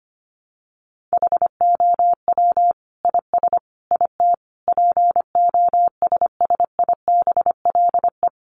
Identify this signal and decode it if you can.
Morse code, heard as a keyed tone, about 25 wpm, 700 Hz. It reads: HOW IS IT POSSIBLE
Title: Sentences from Top 900 Words 25wpm